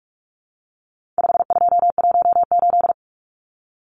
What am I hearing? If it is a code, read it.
Morse code, heard as a keyed tone, about 44 wpm, 700 Hz. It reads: 5218
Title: Numbers - 4-Digit 45wpm